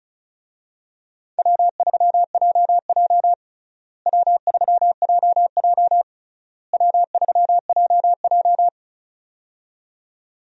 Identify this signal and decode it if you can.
Morse code, heard as a keyed tone, about 36 wpm, 700 Hz. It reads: W3JJ W3JJ W3JJ
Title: USA Call Signs - 4 Characters - Repeated 3x - 35wpm